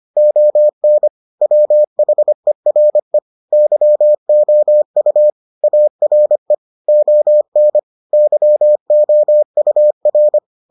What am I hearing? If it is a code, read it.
Morse code, heard as a keyed tone, about 25 wpm, 600 Hz. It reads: ON WHERE YOU ARE ON YOUR